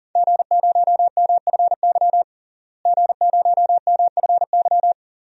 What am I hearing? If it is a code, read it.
Morse code, heard as a keyed tone, about 40 wpm, 700 Hz. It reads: G0MFY G0MFY
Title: DXCC-100 Call Signs - 5 Characters - Repeated 3x - 40wpm